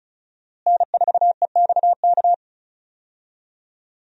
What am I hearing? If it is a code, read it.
Morse code, heard as a keyed tone, about 34 wpm, 700 Hz. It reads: N4EXK
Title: USA Call Signs - 5 Characters - Repeated 2x - 35wpm